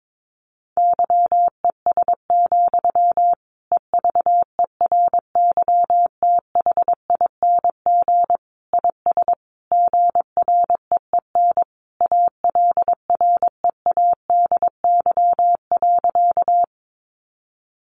Morse code, heard as a keyed tone, about 22 wpm, 700 Hz. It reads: YES, EVERYTHING IS GREEN ALREADY.
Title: Sentences from Top 700 Words 22wpm